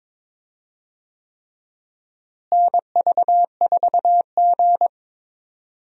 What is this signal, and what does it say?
Morse code, heard as a keyed tone, about 22 wpm, 700 Hz. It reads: NV4G
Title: USA Call Signs - 4 Characters - Repeated 3x - 22wpm